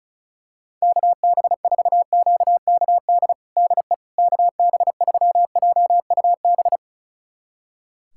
Morse code, heard as a keyed tone, about 36 wpm, 700 Hz. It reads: KB4QKD DE KB3JUB
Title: USA Callsign DE Callsign - 6 Characters - 35wpm